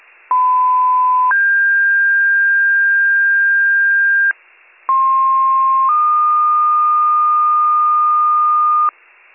Two-tone page, heard as a one-second tone, then a three-second tone, about 1010 then 1600 Hz, about 1060 then 1220 Hz, with noise. It is heard from 0.3 to 8.9 s.